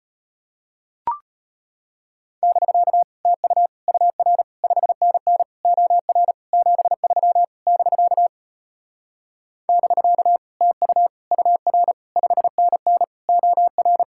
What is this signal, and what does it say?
Morse code, heard as a keyed tone, about 33 wpm, 700 Hz. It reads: <BK> TU UR 5NN OR 73 <BK> <BK> TU UR 5NN OR